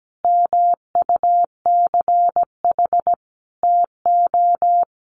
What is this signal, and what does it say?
Morse code, heard as a keyed tone, about 17 wpm, 700 Hz. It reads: MUCH TO
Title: Sentences from Top 1100 Words 17wpm